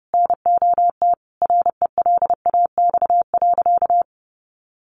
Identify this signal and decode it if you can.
Morse code, heard as a keyed tone, about 30 wpm, 700 Hz. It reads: NOT RELAX.
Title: Sentences from Top 1400 Words 30wpm